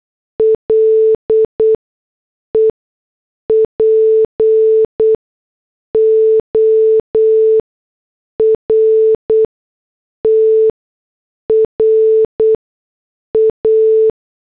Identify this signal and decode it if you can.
Morse code, heard as a keyed tone, about 8 wpm, spaced as for 6 wpm, 440 Hz. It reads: LE PORTRA
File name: morse / headlines